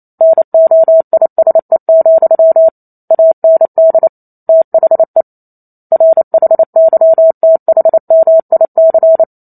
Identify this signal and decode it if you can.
Morse code, heard as a keyed tone, about 28 wpm, 650 Hz. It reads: NOISE, AND THE RHYTHMIC